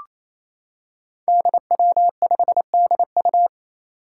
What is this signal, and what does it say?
Morse code, heard as a keyed tone, about 28 wpm, 700 Hz. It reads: DW5DU
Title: Five Challenging Letter-Number Combinations 28wpm